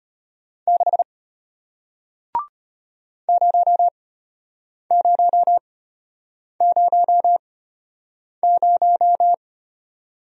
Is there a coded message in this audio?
Yes: Morse code from 0.7 to 9.3 s, heard as a keyed tone, about 34 wpm, spaced as for 8 wpm, 700 Hz. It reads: 6 0000